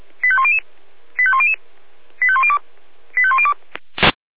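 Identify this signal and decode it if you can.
Five-tone selective call (ZVEI), heard as quick runs of five tones, tones of about 70 ms. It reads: 85100 85100 85102 85102
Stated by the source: ZVEI five-tone selective call received on an SDR.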